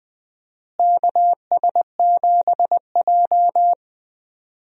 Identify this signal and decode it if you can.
Morse code, heard as a keyed tone, about 20 wpm, 700 Hz. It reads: KS7J
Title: USA Call Signs - 4 Characters - Repeated 2x - 20wpm